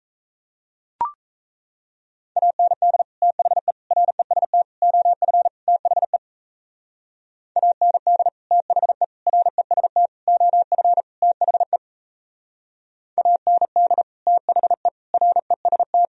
Morse code, heard as a keyed tone, about 36 wpm, 700 Hz. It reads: AND THE REST OF THE AND THE REST OF THE AND THE REST